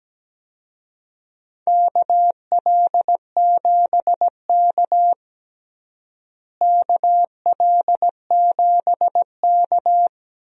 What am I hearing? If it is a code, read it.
Morse code, heard as a keyed tone, about 17 wpm, 700 Hz. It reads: KL7K KL7K
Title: DXCC-100 Call Signs - 4 Characters - Repeated 2x - 17wpm